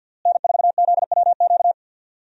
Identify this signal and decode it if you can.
Morse code, heard as a keyed tone, about 50 wpm, 700 Hz. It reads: N4ZWQ
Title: USA Call Signs - 5 Characters - Repeated 3x - 50wpm